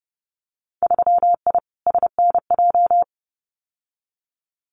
Morse code, heard as a keyed tone, about 32 wpm, 700 Hz. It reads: 3I SNJ
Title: Winter Field Day Exchanges Encore 30wpm